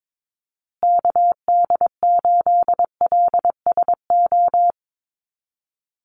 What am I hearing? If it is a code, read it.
Morse code, heard as a keyed tone, about 22 wpm, 700 Hz. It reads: KD8LSO
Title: USA Call Signs Repeated 2x 22wpm